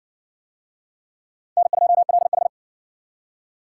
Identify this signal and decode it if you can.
Morse code, heard as a keyed tone, about 60 wpm, 700 Hz. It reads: N2LH